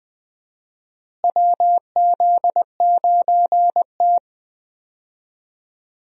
Morse code, heard as a keyed tone, about 20 wpm, 700 Hz. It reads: WZ9T
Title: USA Call Signs - 4 Characters - Repeated 2x - 20wpm